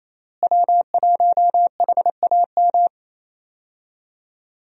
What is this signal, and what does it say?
Morse code, heard as a keyed tone, about 28 wpm, 700 Hz. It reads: W1HAM
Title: USA Call Signs - 5 Characters - Repeated 2x - 28wpm